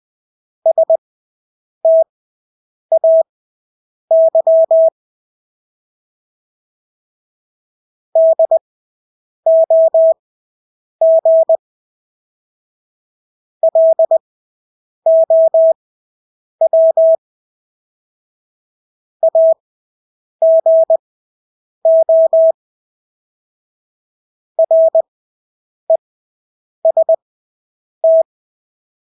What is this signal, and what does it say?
Morse code, heard as a keyed tone, about 20 wpm, spaced as for 8 wpm, 650 Hz. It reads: STAY DOG LOW AGO REST
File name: Engelske ord 20/8 wpm